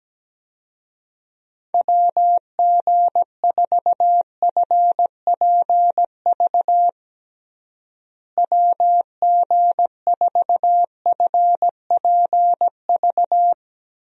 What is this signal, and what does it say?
Morse code, heard as a keyed tone, about 17 wpm, 700 Hz. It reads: WG4FPV WG4FPV
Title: USA Call Signs - 6 Characters - Repeated 2x - 17wpm